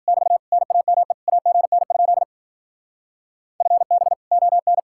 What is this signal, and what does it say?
Morse code, heard as a keyed tone, about 49 wpm, 700 Hz. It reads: <BT> NAME AGN? FB ON
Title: Speed-Racing Common QSO Phrases Encore - 40wpm